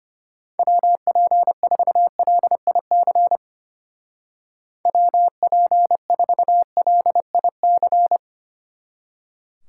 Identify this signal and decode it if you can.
Morse code, heard as a keyed tone, about 32 wpm, 700 Hz. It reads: WP4LIC WP4LIC